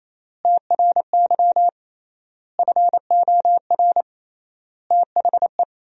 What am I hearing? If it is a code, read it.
Morse code, heard as a keyed tone, about 28 wpm, 700 Hz. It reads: TRY FOR THE
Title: 3x Word-Spacing - Sentences from Top 400 Words - 28wpm